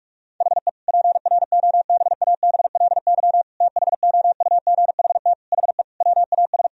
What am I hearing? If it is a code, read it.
Morse code, heard as a keyed tone, about 46 wpm, 700 Hz. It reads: HE PROBABLY THOUGHT HE WAS
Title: Sentences from Top 1000 Words - Review Part 4 - 45wpm